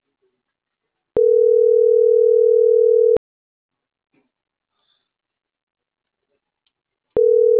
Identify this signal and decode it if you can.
Ringback tone (North America), 440 + 480 Hz.